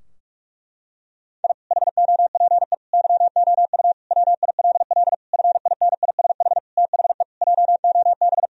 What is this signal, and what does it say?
Morse code, heard as a keyed tone, about 42 wpm, 700 Hz. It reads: I HOPE YOU WILL FINISH THE JOB
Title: Sentences from Top 1000 Words - Review Part 3 - 45wpm